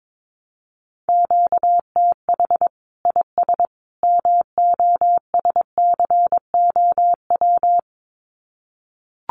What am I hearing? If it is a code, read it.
Morse code, heard as a keyed tone, about 22 wpm, 700 Hz. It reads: QTH IS MOSCOW
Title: Common QSO Phrases 22wpm